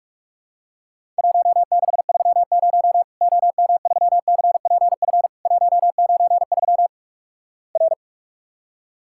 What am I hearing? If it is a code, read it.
Morse code, heard as a keyed tone, about 46 wpm, 700 Hz. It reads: 1630 OM3CPF 193 R
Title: Worked All Europe DX Contest - QTC Exchange - 45wpm